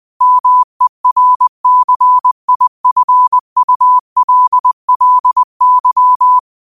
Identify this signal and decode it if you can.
Morse code, heard as a keyed tone, about 20 wpm, 1000 Hz. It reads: MERCIFULLY